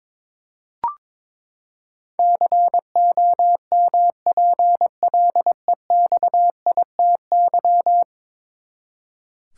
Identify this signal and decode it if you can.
Morse code, heard as a keyed tone, about 22 wpm, 700 Hz. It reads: COMPLEXITY
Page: morse-code-ninja-practice-22wpm